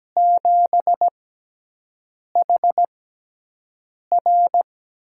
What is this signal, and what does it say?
Morse code, heard as a keyed tone, about 17 wpm, spaced as for 6 wpm, 700 Hz. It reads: 7HR
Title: USA Call Signs - 4 Characters - 6x Character Spacing - 17wpm